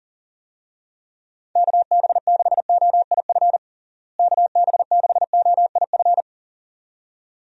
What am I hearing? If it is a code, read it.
Morse code, heard as a keyed tone, about 37 wpm, 700 Hz. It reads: KB6OIF KB6OIF
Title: USA Call Signs - 6 Characters - Repeated 2x - 40wpm